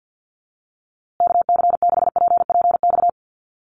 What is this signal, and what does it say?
Morse code, heard as a keyed tone, about 50 wpm, 700 Hz. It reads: KC6PPX
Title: USA Call Signs Repeated 3x 50wpm